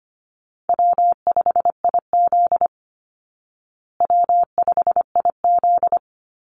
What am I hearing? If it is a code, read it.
Morse code, heard as a keyed tone, about 25 wpm, 700 Hz. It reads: W5IZ W5IZ
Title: Easy Call Signs Repeated 3x - 25wpm